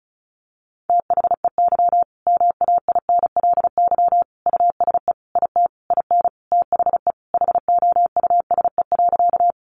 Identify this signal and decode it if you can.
Morse code, heard as a keyed tone, about 34 wpm, 700 Hz. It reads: THEY MAINLY USE IT IN THE HOUSE.
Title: Sentences from Top 1400 Words 35wpm